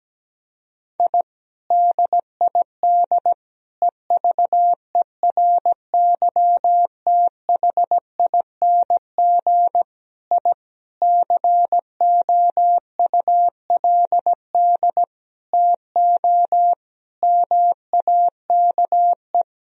Morse code, heard as a keyed tone, about 17 wpm, 700 Hz. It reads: I DID EVERYTHING I COULD TO MAKE
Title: Sentences from Top 1000 Words - Review Part 1 - 17wpm